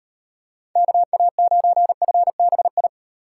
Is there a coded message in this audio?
Yes: Morse code from 0.8 to 2.9 s, heard as a keyed tone, about 38 wpm, 700 Hz. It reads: KA9FBI